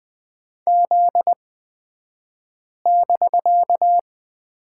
Morse code, heard as a keyed tone, about 20 wpm, spaced as for 5 wpm, 700 Hz. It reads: Z<BK>
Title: 249 Lesson 40d - BK - Two Characters 20wpm